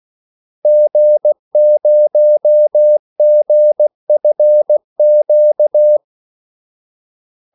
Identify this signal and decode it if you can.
Morse code, heard as a keyed tone, about 16 wpm, 600 Hz. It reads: G0GFQ
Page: Kaldesignaler 16 wpm